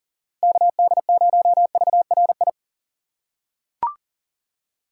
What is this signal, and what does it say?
Morse code, heard as a keyed tone, about 40 wpm, 700 Hz. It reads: KD0VRI
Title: USA Call Signs - 6 Characters - 4x Character Spacing - 40wpm